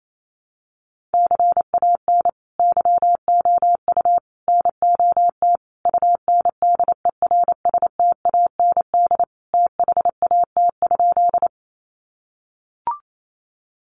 Morse code, heard as a keyed tone, about 28 wpm, 700 Hz. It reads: CAN YOU NOT UNDERSTAND THAT?
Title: Sentences from Top 600 Words 28wpm